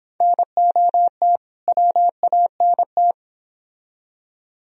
Morse code, heard as a keyed tone, about 28 wpm, 700 Hz. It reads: NOT WANT
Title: Speed-Racing - Sets of 2 Words 22wpm